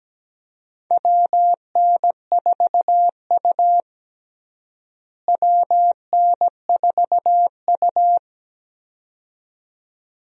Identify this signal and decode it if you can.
Morse code, heard as a keyed tone, about 17 wpm, 700 Hz. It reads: WN4U WN4U
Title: USA Call Signs - 4 Characters - Repeated 2x - 17wpm